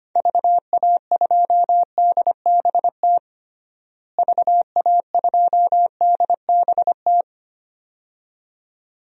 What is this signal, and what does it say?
Morse code, heard as a keyed tone, about 25 wpm, 700 Hz. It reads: VA2DBT VA2DBT
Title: DXCC-100 Call Signs - 6 Characters - Repeated 2x - 25wpm